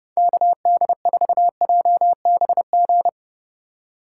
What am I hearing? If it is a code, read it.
Morse code, heard as a keyed tone, about 36 wpm, 700 Hz. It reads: KD4JBG
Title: Speed Racing - USA 2x3 Call Signs 30wpm